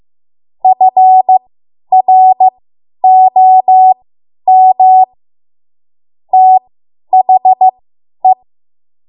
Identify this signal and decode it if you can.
Morse code, heard as a keyed tone, about 15 wpm, spaced as for 10 wpm, 750 Hz. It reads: FROM THE
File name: Morse / 10WPM